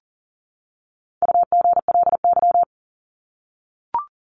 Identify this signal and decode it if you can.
Morse code, heard as a keyed tone, about 40 wpm, 700 Hz. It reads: UGLY
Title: Top 1751-2000 Words 40wpm